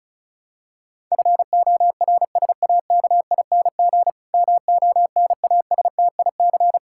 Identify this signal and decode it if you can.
Morse code, heard as a keyed tone, about 36 wpm, 700 Hz. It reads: FORSAKING MONASTIC
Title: Pangram Challenge v2 - 35wpm